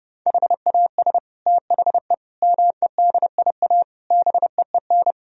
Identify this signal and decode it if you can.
Morse code, heard as a keyed tone, about 30 wpm, 700 Hz. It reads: HAS THE MEDIA BEEN
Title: Sentences from Top 1000 Words - Review Part 4 - 30wpm